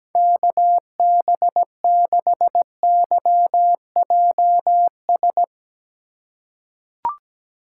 Morse code, heard as a keyed tone, about 17 wpm, 700 Hz. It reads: KB6YJS
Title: USA Call Signs - 3x Character Spacing - 17wpm